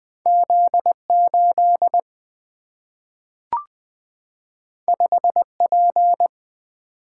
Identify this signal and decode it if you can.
Morse code, heard as a keyed tone, about 20 wpm, 700 Hz. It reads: Z8 5P
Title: Single-Letter Single-Number 20wpm